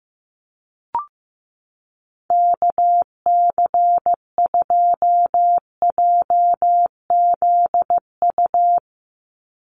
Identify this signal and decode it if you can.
Morse code, heard as a keyed tone, about 15 wpm, 700 Hz. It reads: KC2JZU
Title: USA Call Signs 15wpm